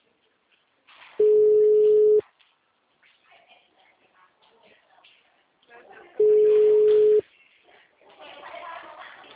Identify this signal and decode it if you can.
Ringback tone (Europe, 425 Hz), with speech.